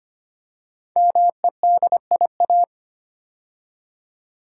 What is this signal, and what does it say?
Morse code, heard as a keyed tone, about 25 wpm, 700 Hz. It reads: MEDIA
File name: 5 Letter Words - First 3 Letters Given - 25wpm